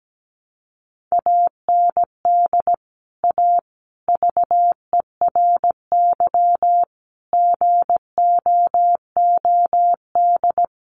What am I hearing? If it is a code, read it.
Morse code, heard as a keyed tone, about 17 wpm, 700 Hz. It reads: AND A VERY GOOD
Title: Sentences from Top 200 Words 17wpm